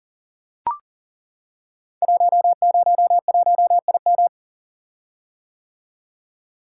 Morse code, heard as a keyed tone, about 39 wpm, 700 Hz. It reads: 101IM
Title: Code Groups - 5 Letter:Numbers - 40wpm